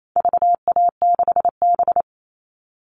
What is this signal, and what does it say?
Morse code, heard as a keyed tone, about 28 wpm, 700 Hz. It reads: VA6B
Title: All Cut Numbers - 4-Digit 28wpm